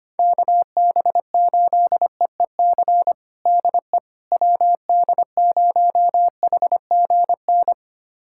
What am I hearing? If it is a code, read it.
Morse code, heard as a keyed tone, about 25 wpm, 700 Hz. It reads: KB8EEC DE WD0HGN
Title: USA Callsign DE Callsign - 6 Characters - 25wpm